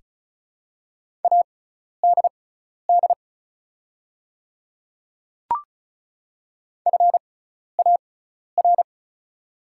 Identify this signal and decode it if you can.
Morse code, heard as a keyed tone, about 36 wpm, spaced as for 12 wpm, 690 Hz. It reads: ADD FAR
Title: 6x Character Spacing - Top 300 Words - 35wpm